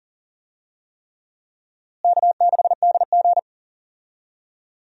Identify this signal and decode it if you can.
Morse code, heard as a keyed tone, about 39 wpm, 700 Hz. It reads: K6DG
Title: USA Call Signs - 4 Characters - Repeated 3x - 40wpm